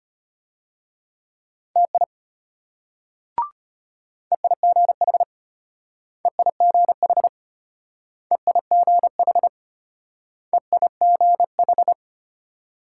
Morse code, heard as a keyed tone, about 32 wpm, 700 Hz. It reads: TI EIGH EIGH EIGH EIGH